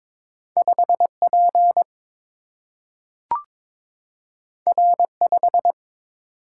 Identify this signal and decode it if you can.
Morse code, heard as a keyed tone, about 22 wpm, 700 Hz. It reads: 5P R5
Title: Single-Letter Single-Number 22wpm